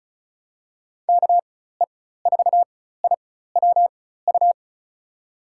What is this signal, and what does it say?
Morse code, heard as a keyed tone, about 34 wpm, spaced as for 16 wpm, 690 Hz. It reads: KE4IWU
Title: USA Call Signs - 4x Character Spacing - 35wpm